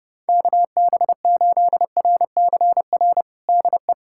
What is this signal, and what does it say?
Morse code, heard as a keyed tone, about 30 wpm, 700 Hz. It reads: KB8RCR DE
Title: USA Callsign DE Callsign - 6 Characters - 30wpm